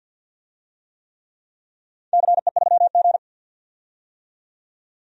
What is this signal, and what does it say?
Morse code, heard as a keyed tone, about 52 wpm, 700 Hz. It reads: CE3G
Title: DXCC-100 Call Signs - 4 Characters - 50wpm